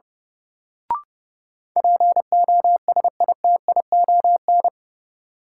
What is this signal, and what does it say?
Morse code, heard as a keyed tone, about 30 wpm, 700 Hz. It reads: POSITION
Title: Sentences from Top 900 Words - New Word Drill - 30wpm